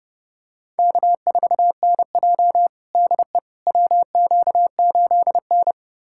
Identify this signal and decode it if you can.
Morse code, heard as a keyed tone, about 30 wpm, 700 Hz. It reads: K4NJ DE WQ8N
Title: USA Callsign DE Callsign - 4 Characters - 30wpm